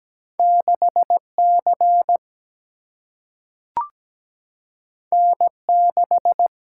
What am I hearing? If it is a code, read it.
Morse code, heard as a keyed tone, about 17 wpm, 700 Hz. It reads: 6 C N 6
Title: Single-Letter Single-Number 17wpm